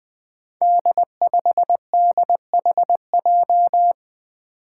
Morse code, heard as a keyed tone, about 20 wpm, 700 Hz. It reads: D5DHJ
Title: Five Challenging Letter-Number Combinations 20wpm